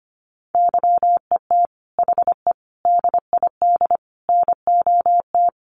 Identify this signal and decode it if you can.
Morse code, heard as a keyed tone, about 25 wpm, 700 Hz. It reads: YET HE DID NOT
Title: Sentences from Top 800 Words 25wpm